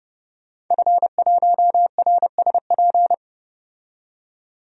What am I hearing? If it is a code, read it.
Morse code, heard as a keyed tone, about 30 wpm, 700 Hz. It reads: F1RSP
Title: DXCC-100 Call Signs - 5 Characters - 30wpm